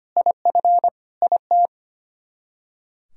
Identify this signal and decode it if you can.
Morse code, heard as a keyed tone, about 33 wpm, 700 Hz. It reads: IF IT